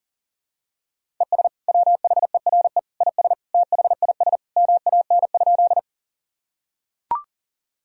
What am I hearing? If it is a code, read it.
Morse code, heard as a keyed tone, about 40 wpm, 700 Hz. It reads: ES WHERE IS THIS MAN?
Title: Sentences from Top 200 Words and Abbreviations 40wpm